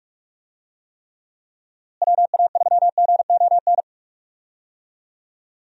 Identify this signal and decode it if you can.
Morse code, heard as a keyed tone, about 46 wpm, 700 Hz. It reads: WA3GON
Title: USA Call Signs - 6 Characters - Repeated 2x - 45wpm